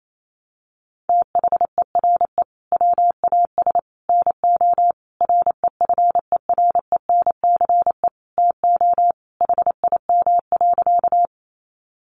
Morse code, heard as a keyed tone, about 28 wpm, 700 Hz. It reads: THERE WAS NO REFERENCE TO HIM.
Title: Sentences from Top 1200 Words 28wpm